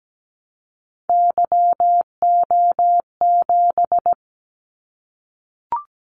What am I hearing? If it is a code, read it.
Morse code, heard as a keyed tone, about 17 wpm, 700 Hz. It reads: YO7
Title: Callsign Prefixes for Top 25 DXCC - 17wpm